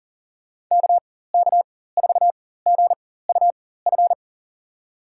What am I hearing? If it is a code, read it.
Morse code, heard as a keyed tone, about 39 wpm, spaced as for 18 wpm, 700 Hz. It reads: KK4GUF
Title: USA Call Signs - 6 Characters - 4x Character Spacing - 40wpm